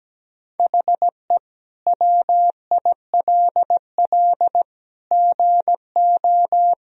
Morse code, heard as a keyed tone, about 17 wpm, 700 Hz. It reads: HE WILL GO
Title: Sentences from Top 1000 Words - Review Part 1 - 17wpm